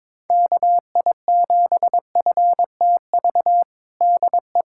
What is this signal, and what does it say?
Morse code, heard as a keyed tone, about 21 wpm, 700 Hz. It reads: KI7FTV DE
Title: USA Callsign DE Callsign - 6 Characters - 22wpm